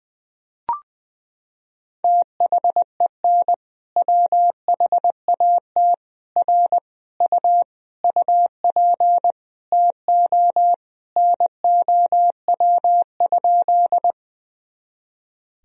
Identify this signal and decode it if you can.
Morse code, heard as a keyed tone, about 20 wpm, 700 Hz. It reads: THEN WHAT R U UP TO NOW?
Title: Sentences from Top 100 Words and Abbreviations 20wpm